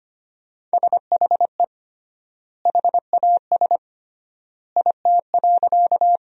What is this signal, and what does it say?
Morse code, heard as a keyed tone, about 25 wpm, 700 Hz. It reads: SHE HAS IT.
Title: 3x Word-Spacing - Sentences from Top 100 Words - 25wpm